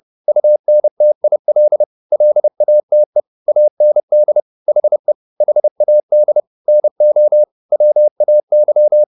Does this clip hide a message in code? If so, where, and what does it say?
Yes: Morse code from 0.3 to 9.0 s, heard as a keyed tone, about 30 wpm, 600 Hz. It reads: UNTIL LATE AND HE HAD NO WAY